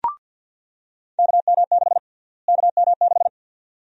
Morse code, heard as a keyed tone, about 48 wpm, 700 Hz. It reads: KM6 KM6
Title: USA Call Sign Prefixes - Repeated 2x - 50wpm